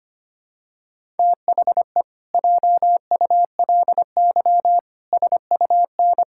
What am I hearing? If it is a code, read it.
Morse code, heard as a keyed tone, about 25 wpm, 700 Hz. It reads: THE JULY SUN
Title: Pangram Challenge v2 - 25wpm